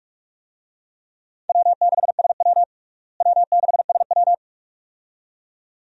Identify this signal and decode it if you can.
Morse code, heard as a keyed tone, about 46 wpm, 700 Hz. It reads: W6SW W6SW
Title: USA Call Signs - 4 Characters - Repeated 2x - 45wpm